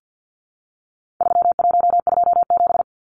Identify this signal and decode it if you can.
Morse code, heard as a keyed tone, about 50 wpm, 700 Hz. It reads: 3127
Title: Numbers - 4-Digit 50wpm